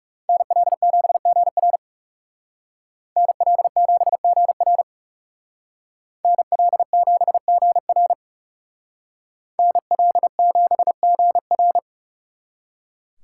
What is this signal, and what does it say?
Morse code, heard as a keyed tone, about 37 wpm, 700 Hz. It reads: NL7GR NL7GR NL7GR NL7GR
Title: Speed Racing - USA 2x2 Call Signs 30wpm